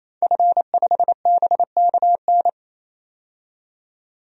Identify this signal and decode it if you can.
Morse code, heard as a keyed tone, about 28 wpm, 700 Hz. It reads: F5BKN
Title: DXCC-100 Call Signs - 5 Characters - 28wpm